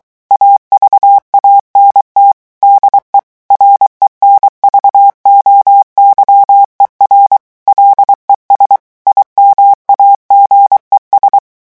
Morse code, heard as a keyed tone, about 23 wpm, 800 Hz. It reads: AVANT DE RENVOYER LES IMAGES
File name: morse mp3 / revision9 citations humoristiques vitesse 22 mots minute